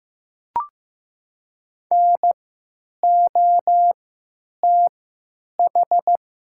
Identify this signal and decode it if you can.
Morse code, heard as a keyed tone, about 15 wpm, spaced as for 9 wpm, 700 Hz. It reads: NOTH
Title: 3x Character Spacing - Top 500 Words - 15wpm